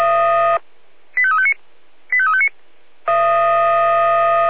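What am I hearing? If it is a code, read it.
Five-tone selective call (ZVEI), heard as quick runs of five tones, tones of about 70 ms. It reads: 85359 85359
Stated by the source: ZVEI five-tone selective call received on an SDR.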